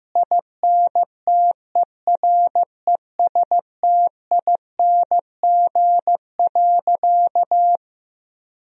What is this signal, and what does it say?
Morse code, heard as a keyed tone, about 15 wpm, 700 Hz. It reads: INTERESTING.
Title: Sentences from Top 1000 Words - Review Part 1 - 15wpm